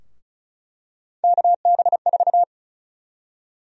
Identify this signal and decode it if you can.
Morse code, heard as a keyed tone, about 34 wpm, 700 Hz. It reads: KB4
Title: USA Call Sign Prefixes - 35wpm